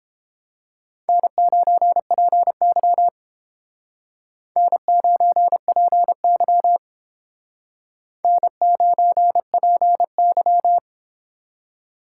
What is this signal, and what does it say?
Morse code, heard as a keyed tone, about 31 wpm, 700 Hz. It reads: N9PY N9PY N9PY